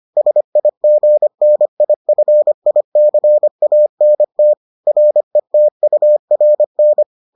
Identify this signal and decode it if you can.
Morse code, heard as a keyed tone, about 25 wpm, 600 Hz. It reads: SIGNIFICANT RETURN